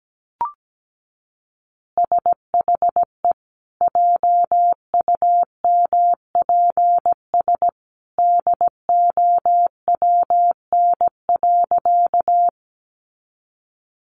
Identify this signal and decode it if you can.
Morse code, heard as a keyed tone, about 17 wpm, 700 Hz. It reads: SHE JUMPS DOWN.
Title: Easy Phrases 17wpm